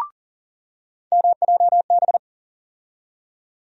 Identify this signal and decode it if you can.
Morse code, heard as a keyed tone, about 38 wpm, 700 Hz. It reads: MJB
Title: USA Call Sign Postfixes - 3 Letters 40wpm